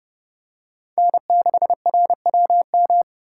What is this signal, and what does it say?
Morse code, heard as a keyed tone, about 29 wpm, 700 Hz. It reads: N6RWM